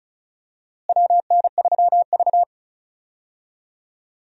Morse code, heard as a keyed tone, about 36 wpm, 700 Hz. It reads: WN3V
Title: USA Call Signs - 4 Characters - Repeated 3x - 35wpm